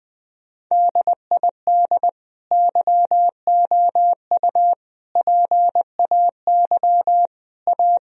Morse code, heard as a keyed tone, about 20 wpm, 700 Hz. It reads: DID YOU PAY A
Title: Sentences from Top 1000 Words - Review Part 4 - 20wpm